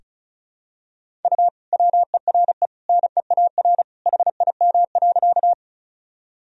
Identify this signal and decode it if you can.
Morse code, heard as a keyed tone, about 34 wpm, 700 Hz. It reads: U WERE NEAR HIM.
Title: Sentences from Top 200 Words and Abbreviations 35wpm